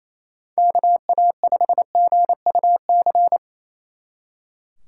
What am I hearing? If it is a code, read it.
Morse code, heard as a keyed tone, about 34 wpm, 700 Hz. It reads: KA5GUC
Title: Speed Racing - USA 2x3 Call Signs 28wpm